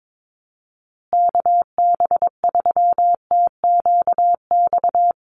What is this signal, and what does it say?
Morse code, heard as a keyed tone, about 22 wpm, 700 Hz. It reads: KB3TQX
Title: USA Call Signs Repeated 2x 22wpm